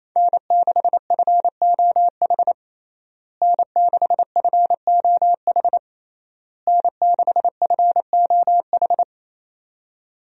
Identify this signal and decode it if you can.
Morse code, heard as a keyed tone, about 28 wpm, 700 Hz. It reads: N6FOH N6FOH N6FOH
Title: USA Call Signs - 5 Characters - Repeated 3x - 28wpm